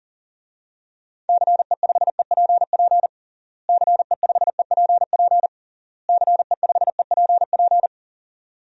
Morse code, heard as a keyed tone, about 40 wpm, 700 Hz. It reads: CE5EPP CE5EPP CE5EPP
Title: DXCC-100 Call Signs - 6 Characters - Repeated 3x - 40wpm